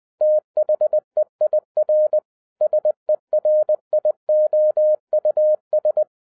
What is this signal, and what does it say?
Morse code, heard as a keyed tone, about 20 wpm, 610 Hz. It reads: THEIR SERIOUS
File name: 2011-07-mp3 / n33 THE 3.11 2 20wpm